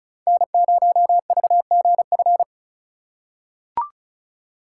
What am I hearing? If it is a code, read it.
Morse code, heard as a keyed tone, about 36 wpm, 700 Hz. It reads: N0VGF
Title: USA Call Signs - 2x Character Spacing - 35wpm